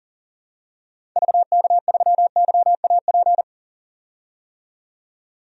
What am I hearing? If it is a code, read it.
Morse code, heard as a keyed tone, about 40 wpm, 700 Hz. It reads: VK3YAP
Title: DXCC-100 Call Signs - 6 Characters - 40wpm